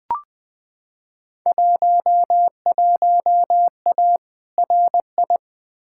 Morse code, heard as a keyed tone, about 20 wpm, 700 Hz. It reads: 11A RI
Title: Field Day Exchanges Encore - 20wpm